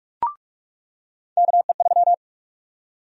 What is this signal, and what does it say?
Morse code, heard as a keyed tone, about 46 wpm, 700 Hz. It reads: KE3
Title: USA Call Sign Prefixes - 45wpm